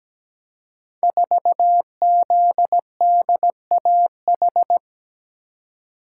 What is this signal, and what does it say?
Morse code, heard as a keyed tone, about 17 wpm, 700 Hz. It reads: 4ZDAH
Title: Code Groups - 5 Letter:Numbers - 17wpm